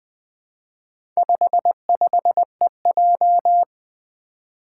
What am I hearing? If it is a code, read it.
Morse code, heard as a keyed tone, about 20 wpm, 700 Hz. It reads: 55EJ
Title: Four Challenging Letter-Number Combinations 20wpm